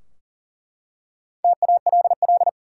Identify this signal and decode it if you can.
Morse code, heard as a keyed tone, about 40 wpm, 700 Hz. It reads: TALL
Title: Four Letter Words - 40wpm